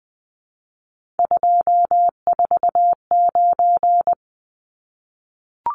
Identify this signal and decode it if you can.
Morse code, heard as a keyed tone, about 20 wpm, 700 Hz. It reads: 249
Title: Numbers - 3-Digit 20wpm